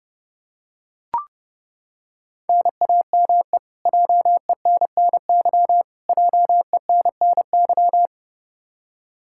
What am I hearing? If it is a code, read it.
Morse code, heard as a keyed tone, about 32 wpm, 700 Hz. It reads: NAME JENNY JENNY
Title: Practice QSO - Exchange 1 - Beginning - 30wpm